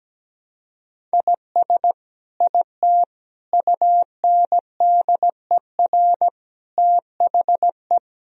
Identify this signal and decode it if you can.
Morse code, heard as a keyed tone, about 17 wpm, 700 Hz. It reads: IS IT UNDER THE
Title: Sentences from Top 1000 Words - Review Part 1 - 17wpm